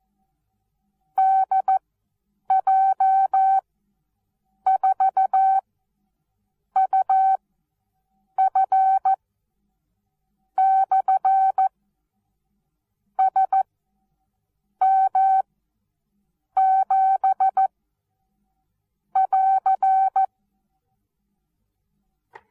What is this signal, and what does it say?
Morse code, heard as a keyed tone, about 14 wpm, spaced as for 6 wpm, 770 Hz. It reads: DJ4UF/SM7<AR>
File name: Onlinelehrgang / morsen / DJ4UF Morsen19